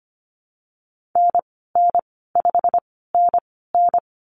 Morse code, heard as a keyed tone, about 25 wpm, spaced as for 16 wpm, 700 Hz. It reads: NN5NN
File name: Lesson 15g - Number 5 - Call Signs 25wpm